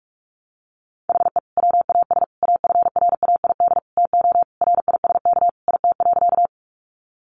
Morse code, heard as a keyed tone, about 44 wpm, 700 Hz. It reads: HE WAS AFRAID TO RISK IT.
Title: Sentences from Top 1500 Words 45wpm